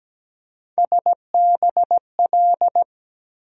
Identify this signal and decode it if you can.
Morse code, heard as a keyed tone, about 17 wpm, 700 Hz. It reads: SBL
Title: USA Call Sign Postfixes - 3 Letters 17wpm